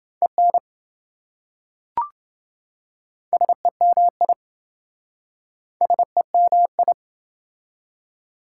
Morse code, heard as a keyed tone, about 28 wpm, 700 Hz. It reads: EN SEMI SEMI